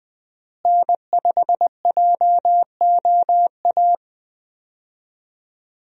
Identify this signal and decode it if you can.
Morse code, heard as a keyed tone, about 20 wpm, 700 Hz. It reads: N5JOA
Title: USA Call Signs - 5 Characters - Repeated 3x - 20wpm